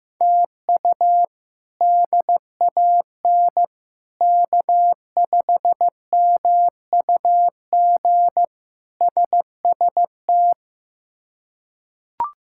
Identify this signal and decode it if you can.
Morse code, heard as a keyed tone, about 16 wpm, 700 Hz. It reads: TU DAN K5MUG SST
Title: K1USN SST QSO Practice - 15wpm